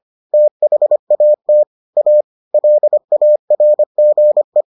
Morse code, heard as a keyed tone, about 25 wpm, 600 Hz. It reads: THAT A LARGE